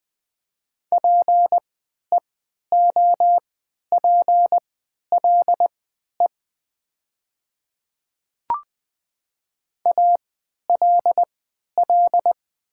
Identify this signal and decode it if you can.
Morse code, heard as a keyed tone, about 20 wpm, spaced as for 11 wpm, 700 Hz. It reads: PEOPLE ALL
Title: 3x Character Spacing - Top 100 Words - 20wpm